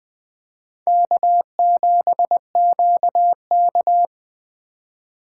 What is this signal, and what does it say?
Morse code, heard as a keyed tone, about 20 wpm, 700 Hz. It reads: K7QK
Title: Speed Racing - USA 1x2 Call Signs 17wpm